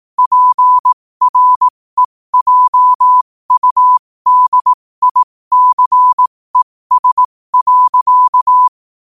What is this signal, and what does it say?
Morse code, heard as a keyed tone, about 18 wpm, spaced as for 15 wpm, 1000 Hz. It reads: PREJUDICES.